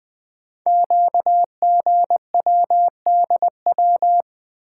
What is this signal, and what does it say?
Morse code, heard as a keyed tone, about 20 wpm, 700 Hz. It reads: QGWDW
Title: Code Groups - 5 Letters - 20wpm